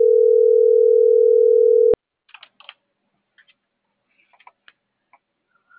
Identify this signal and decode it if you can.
Ringback tone (North America), 440 + 480 Hz.